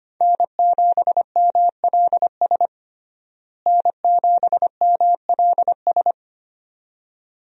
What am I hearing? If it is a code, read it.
Morse code, heard as a keyed tone, about 25 wpm, 700 Hz. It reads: N7MLS N7MLS
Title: USA Call Signs - 5 Characters - Repeated 2x - 25wpm